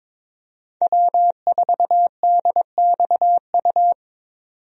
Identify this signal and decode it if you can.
Morse code, heard as a keyed tone, about 22 wpm, 700 Hz. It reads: W4DXU
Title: USA Call Signs - 5 Characters - 22wpm